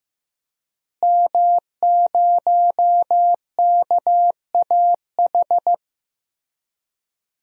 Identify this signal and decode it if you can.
Morse code, heard as a keyed tone, about 15 wpm, 700 Hz. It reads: M0KAH
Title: DXCC-100 Call Signs - 5 Characters - Repeated 3x - 15wpm